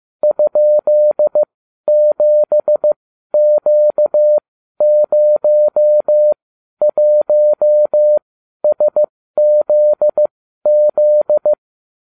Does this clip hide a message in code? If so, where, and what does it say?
Yes: Morse code from 0.2 to 11.5 s, heard as a keyed tone, about 15 wpm, spaced as for 12 wpm, 600 Hz. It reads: ?7Q01SZZ